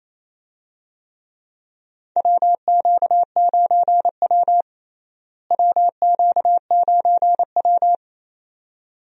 Morse code, heard as a keyed tone, about 28 wpm, 700 Hz. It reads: WQ9W WQ9W
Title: USA Call Signs - 4 Characters - Repeated 2x - 28wpm